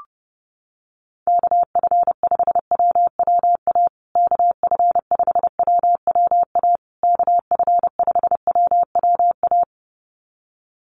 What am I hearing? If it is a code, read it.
Morse code, heard as a keyed tone, about 30 wpm, 700 Hz. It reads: KF5WWA KF5WWA KF5WWA
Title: USA Call Signs Repeated 3x 30wpm